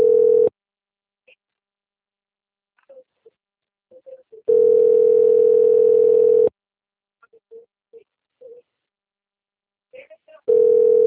Ringback tone (North America), 440 + 480 Hz.